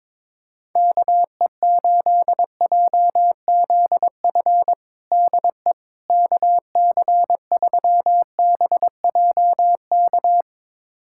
Morse code, heard as a keyed tone, about 22 wpm, 700 Hz. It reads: KE8JZF DE KC3BJK
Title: USA Callsign DE Callsign - 6 Characters - 22wpm